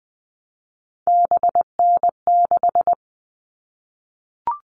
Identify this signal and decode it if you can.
Morse code, heard as a keyed tone, about 20 wpm, 700 Hz. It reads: BN6
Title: All Cut Numbers - 3-Digit 20wpm